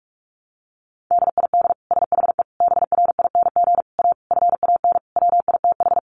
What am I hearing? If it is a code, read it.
Morse code, heard as a keyed tone, about 44 wpm, 700 Hz. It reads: DID SHE BRING A FAN WITH
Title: Sentences from Top 1300 Words 45wpm